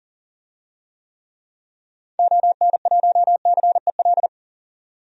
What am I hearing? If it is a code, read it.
Morse code, heard as a keyed tone, about 40 wpm, 700 Hz. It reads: ON1CEL
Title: DXCC-100 Call Signs - 6 Characters - 40wpm